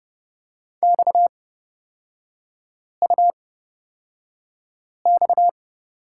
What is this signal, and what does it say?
Morse code, heard as a keyed tone, about 31 wpm, spaced as for 5 wpm, 690 Hz. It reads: XUX
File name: Lesson 37e - Letter x - Three Characters 30wpm